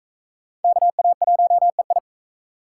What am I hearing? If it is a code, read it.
Morse code, heard as a keyed tone, about 40 wpm, 700 Hz. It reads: KA1EI